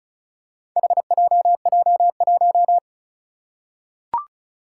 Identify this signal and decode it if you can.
Morse code, heard as a keyed tone, about 36 wpm, 700 Hz. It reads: HJJ1
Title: Four Challenging Letter-Number Combinations 35wpm